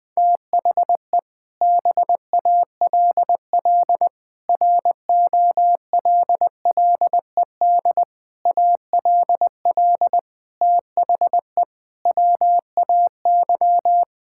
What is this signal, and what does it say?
Morse code, heard as a keyed tone, about 20 wpm, 700 Hz. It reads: THE BALL ROLLED ALL THE WAY
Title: Sentences with 6-Word Phrases - 20wpm